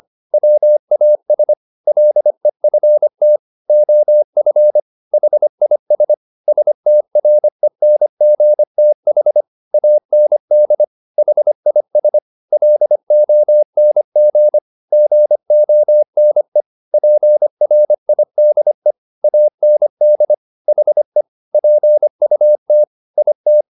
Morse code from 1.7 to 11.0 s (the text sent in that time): LEFT OF HIS STRENGTH AND